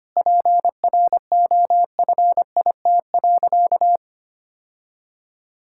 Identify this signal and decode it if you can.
Morse code, heard as a keyed tone, about 25 wpm, 700 Hz. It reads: PROFIT.
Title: Sentences from Top 1000 Words - Review Part 5 - 25wpm